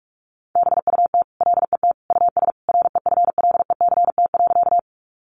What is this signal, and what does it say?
Morse code, heard as a keyed tone, about 44 wpm, 700 Hz. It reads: BUT LET US REFLECT.
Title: Sentences from Top 1500 Words 45wpm